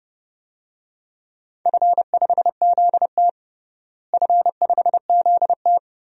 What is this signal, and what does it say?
Morse code, heard as a keyed tone, about 30 wpm, 700 Hz. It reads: F5ZT F5ZT
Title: DXCC-100 Call Signs - 4 Characters - Repeated 3x - 30wpm